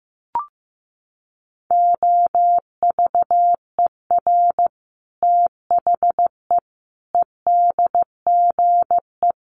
Morse code, heard as a keyed tone, about 15 wpm, 700 Hz. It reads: OVER THE EDGE
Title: Sets of 5 Words Encore2 15wpm